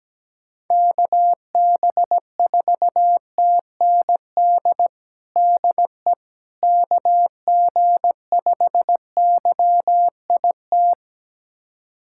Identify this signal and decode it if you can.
Morse code, heard as a keyed tone, about 17 wpm, 700 Hz. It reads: KB4TND DE KG5YIT
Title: USA Callsign DE Callsign - 6 Characters - 17wpm